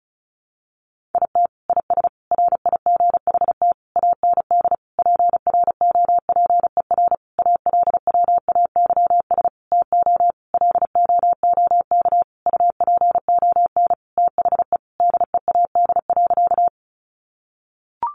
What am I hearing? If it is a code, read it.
Morse code, heard as a keyed tone, about 34 wpm, 700 Hz. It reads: IT IS RIGHT AND PROPER ALWAYS TO LOOK UPON THE DEAD.
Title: Sentences from Top 1500 Words 35wpm